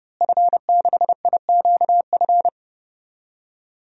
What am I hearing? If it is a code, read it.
Morse code, heard as a keyed tone, about 30 wpm, 700 Hz. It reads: F6IQF
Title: DXCC-100 Call Signs - 5 Characters - Repeated 2x - 30wpm